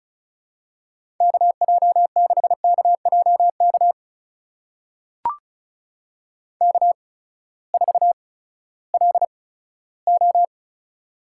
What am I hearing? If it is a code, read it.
Morse code, heard as a keyed tone, about 34 wpm, spaced as for 17 wpm, 700 Hz. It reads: KJ6KJK K4LO
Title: USA Call Signs - 8x Character Spacing - 35wpm